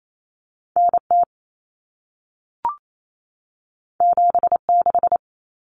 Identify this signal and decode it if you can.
Morse code, heard as a keyed tone, about 28 wpm, 700 Hz. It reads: NT 76
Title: Common Cut Numbers - 2-Digit 28wpm